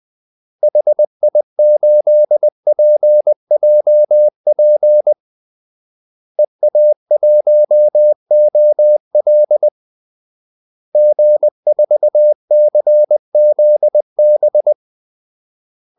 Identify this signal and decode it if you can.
Morse code, heard as a keyed tone, about 20 wpm, 600 Hz. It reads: HI8PJP EA1OL G4CZB
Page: Kaldesignaler 20 wpm